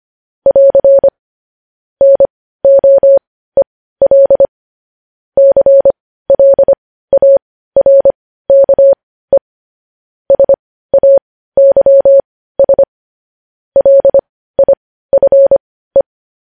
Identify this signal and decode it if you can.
Morse code, heard as a keyed tone, about 25 wpm, spaced as for 15 wpm, 550 Hz. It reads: <AR> NOEL CLARKE SAYS LIFE
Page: news headlines in morse code at 15 wpm